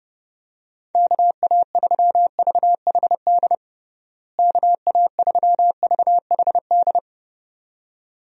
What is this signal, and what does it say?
Morse code, heard as a keyed tone, about 30 wpm, 700 Hz. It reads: KA3VHD KA3VHD
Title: USA Call Signs - 6 Characters - Repeated 2x - 30wpm